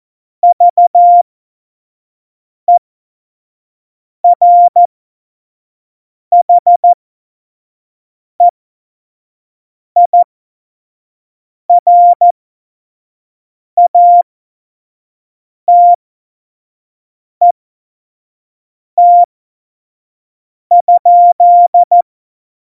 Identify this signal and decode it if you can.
Morse code, heard as a keyed tone, about 14 wpm, spaced as for 5 wpm, 700 Hz. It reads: VERHEIRATET?
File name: Morsekurs / Witze / Witz 01. September 2020, 25 F70